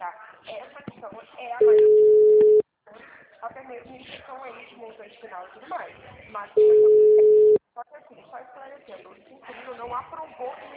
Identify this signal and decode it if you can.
Ringback tone (Europe, 425 Hz), with speech.
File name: out- -2009- - - .28